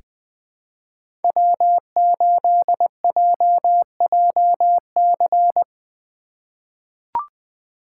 Morse code, heard as a keyed tone, about 20 wpm, 700 Hz. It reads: W8JJC
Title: USA Call Signs - 5 Characters - 4x Character Spacing - 20wpm